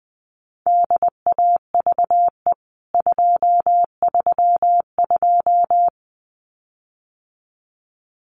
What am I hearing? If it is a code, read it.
Morse code, heard as a keyed tone, about 20 wpm, 700 Hz. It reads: DAVE 232
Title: CWT Exchange 20wpm